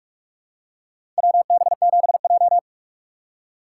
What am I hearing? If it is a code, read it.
Morse code, heard as a keyed tone, about 46 wpm, 700 Hz. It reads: WB7J
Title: USA Call Signs - 4 Characters - 3x Character Spacing - 45wpm